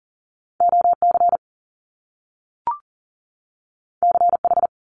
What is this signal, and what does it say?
Morse code, heard as a keyed tone, about 41 wpm, 700 Hz. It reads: OC CH
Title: Top Two Letter Combinations in Words 40wpm